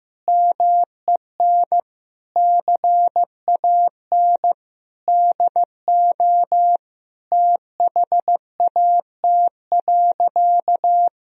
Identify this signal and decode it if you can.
Morse code, heard as a keyed tone, about 15 wpm, 700 Hz. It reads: MEN CAN DO THAT.
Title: Sentences from Top 1000 Words - Review Part 1 - 15wpm